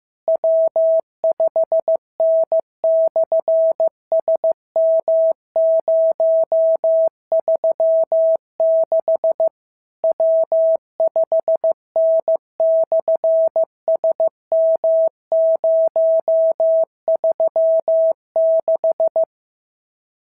Morse code, heard as a keyed tone, about 15 wpm, 650 Hz. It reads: W5N/SM036 W5N/SM036
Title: SOTA QSO Practice 15wpm